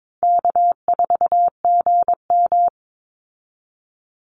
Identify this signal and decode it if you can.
Morse code, heard as a keyed tone, about 22 wpm, 700 Hz. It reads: K4GM
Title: Easy Call Signs Repeated 2x - 22wpm